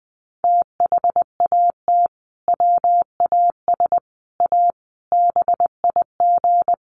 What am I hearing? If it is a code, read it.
Morse code, heard as a keyed tone, about 20 wpm, 700 Hz. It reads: THAT WAS A BIG
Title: Sentences from Top 1400 Words 20wpm